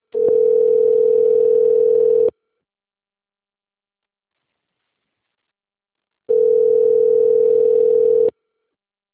Ringback tone (North America), 440 + 480 Hz.